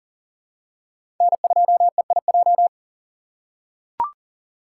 Morse code, heard as a keyed tone, about 39 wpm, 700 Hz. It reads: N2EIJ
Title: USA Call Signs - 5 Characters - 8x Character Spacing - 40wpm